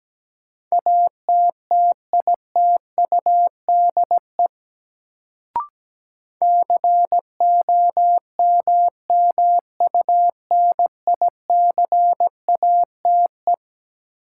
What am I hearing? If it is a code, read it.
Morse code, heard as a keyed tone, about 17 wpm, 700 Hz. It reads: ATTITUDE COMMUNICATE
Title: Sentences from Top 1300 Words - New Word Drill - 17wpm